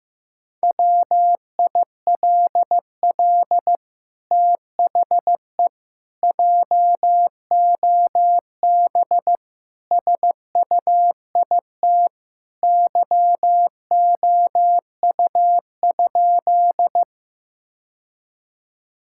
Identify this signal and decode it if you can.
Morse code, heard as a keyed tone, about 15 wpm, 700 Hz. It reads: WILL THE JOB SUIT YOU?
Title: Sentences from Top 1600 Words 15wpm